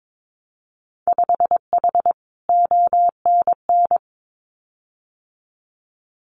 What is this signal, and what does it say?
Morse code, heard as a keyed tone, about 22 wpm, 700 Hz. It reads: 5H ONN
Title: Winter Field Day Exchanges Encore 22wpm